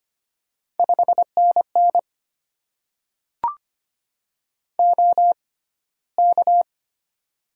Morse code, heard as a keyed tone, about 25 wpm, spaced as for 15 wpm, 690 Hz. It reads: 5NN OK
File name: Top 90 QSO Elements - 6x Character Spacing - 25wpm